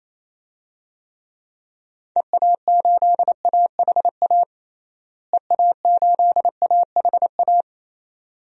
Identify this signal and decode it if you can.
Morse code, heard as a keyed tone, about 28 wpm, 700 Hz. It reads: EA8AHA EA8AHA
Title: DXCC-100 Call Signs - 6 Characters - Repeated 2x - 28wpm